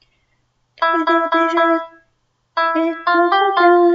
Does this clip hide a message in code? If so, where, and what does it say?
No.